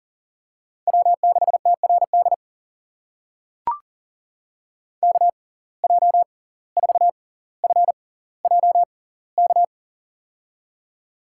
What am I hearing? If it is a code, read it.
Morse code, heard as a keyed tone, about 40 wpm, spaced as for 19 wpm, 700 Hz. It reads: W6TRD KJ4FJK
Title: USA Call Signs - 6x Character Spacing - 40wpm